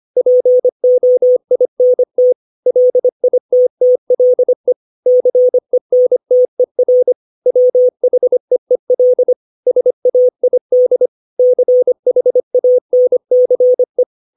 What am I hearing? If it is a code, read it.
Morse code, heard as a keyed tone, about 25 wpm, 500 Hz. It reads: POINT LITTLE CENTER WHEEL SAID CHANCE
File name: 632 words / 632 Words 25WPM